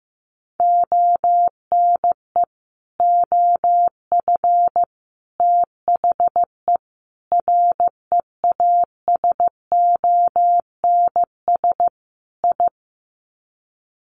Morse code, heard as a keyed tone, about 15 wpm, 700 Hz. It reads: ONE OF THE REASONS I
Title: Sets of 5 Words Encore2 15wpm